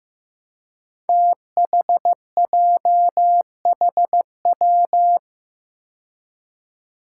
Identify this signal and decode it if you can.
Morse code, heard as a keyed tone, about 16 wpm, 700 Hz. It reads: THJHW
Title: Five Challenging Letter-Number Combinations 15wpm